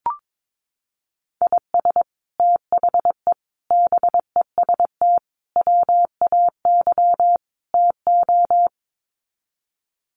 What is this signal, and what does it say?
Morse code, heard as a keyed tone, about 22 wpm, 700 Hz. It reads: IS THE BEST WAY TO
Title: Sets of 5 Words Encore 22wpm